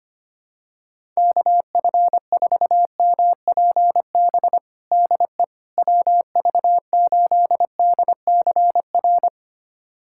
Morse code, heard as a keyed tone, about 25 wpm, 700 Hz. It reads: KF4MPB DE WV8DCR
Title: USA Callsign DE Callsign 25wpm